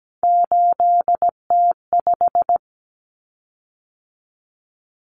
Morse code, heard as a keyed tone, about 17 wpm, 700 Hz. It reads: 8T5
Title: Common Cut Numbers - 3-Digit 17wpm